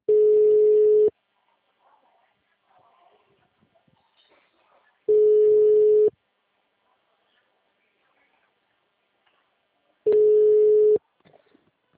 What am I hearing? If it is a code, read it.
Ringback tone (Europe, 425 Hz).